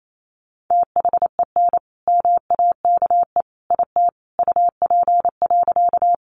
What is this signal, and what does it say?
Morse code, heard as a keyed tone, about 28 wpm, 700 Hz. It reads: THEN MAKE IT UP.
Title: Sentences from Top 100 Words 28wpm